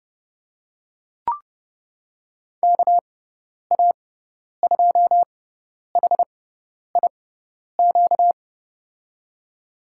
Morse code, heard as a keyed tone, about 31 wpm, spaced as for 10 wpm, 700 Hz. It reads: KA2HIQ
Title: USA Call Signs - 6 Characters - 6x Character Spacing - 30wpm